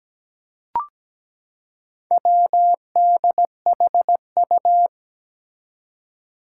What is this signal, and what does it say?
Morse code, heard as a keyed tone, about 17 wpm, 700 Hz. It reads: WDHU
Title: Four Challenging Letter-Number Combinations 17wpm